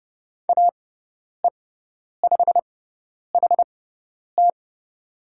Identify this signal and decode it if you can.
Morse code, heard as a keyed tone, about 30 wpm, spaced as for 10 wpm, 700 Hz. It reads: AE5HT
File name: Lesson 16g - Letter c - Call Signs 30wpm